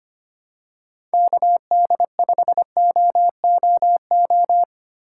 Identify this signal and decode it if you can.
Morse code, heard as a keyed tone, about 25 wpm, 700 Hz. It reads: KD5OOO
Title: USA Call Signs - 6 Characters - 2x Character Spacing - 25wpm